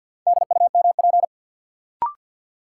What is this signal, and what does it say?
Morse code, heard as a keyed tone, about 52 wpm, 700 Hz. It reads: DUMP
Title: Sentences from Top 1300 Words - New Word Drill - 50wpm